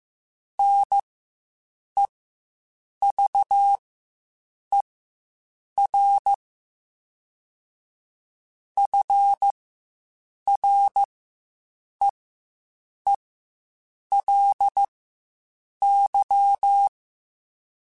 Morse code, heard as a keyed tone, about 15 wpm, spaced as for 7 wpm, 800 Hz. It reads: NEVER FREELY
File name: Quotes Spaced